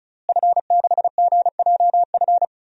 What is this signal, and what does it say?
Morse code, heard as a keyed tone, about 34 wpm, 700 Hz. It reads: F6GJF
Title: DXCC-100 Call Signs - 5 Characters - Repeated 3x - 35wpm